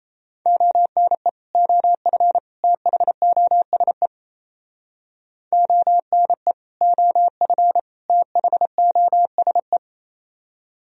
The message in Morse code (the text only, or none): ONE OF THOSE ONE OF THOSE